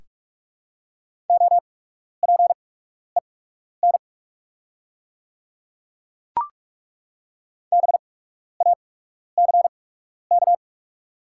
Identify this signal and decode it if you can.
Morse code, heard as a keyed tone, about 48 wpm, spaced as for 12 wpm, 690 Hz. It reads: OPEN BACK
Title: 8x Character Spacing - Top 300 Words - 45wpm